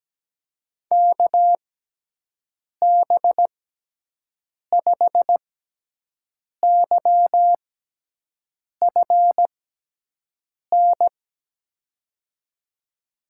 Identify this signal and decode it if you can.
Morse code, heard as a keyed tone, about 17 wpm, spaced as for 6 wpm, 700 Hz. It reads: KB5YFN
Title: USA Call Signs - 6 Characters - 6x Character Spacing - 17wpm